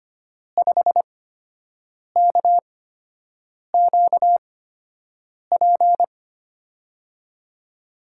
Morse code, heard as a keyed tone, about 25 wpm, spaced as for 7 wpm, 700 Hz. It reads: 5KQP
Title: USA Call Signs - 6 Characters - 8x Character Spacing - 25wpm